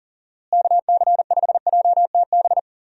Morse code, heard as a keyed tone, about 40 wpm, 700 Hz. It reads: KC5JTB
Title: USA Call Signs - 6 Characters - Repeated 3x - 40wpm